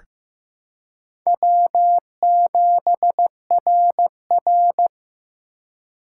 Morse code, heard as a keyed tone, about 15 wpm, 700 Hz. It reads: W7RR
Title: USA Call Signs - 4 Characters - 2x Character Spacing - 15wpm